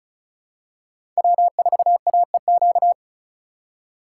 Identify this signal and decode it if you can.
Morse code, heard as a keyed tone, about 36 wpm, 700 Hz. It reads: W4AEQ
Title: USA Call Signs - 5 Characters - 3x Character Spacing - 35wpm